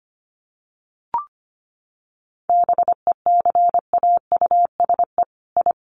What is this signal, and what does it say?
Morse code, heard as a keyed tone, about 25 wpm, 700 Hz. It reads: BECAUSE I
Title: Sets of 2 Words 25wpm